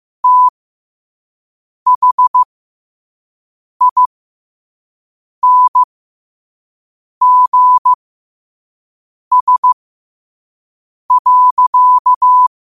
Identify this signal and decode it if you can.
Morse code, heard as a keyed tone, about 15 wpm, spaced as for 5 wpm, 1000 Hz. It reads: THINGS.
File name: qotdfiles / 5wpm